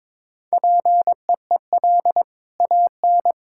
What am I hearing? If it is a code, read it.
Morse code, heard as a keyed tone, about 21 wpm, 700 Hz. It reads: PEEL AN
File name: practice / 22wpm / Easy Sentences with Hints - 22wpm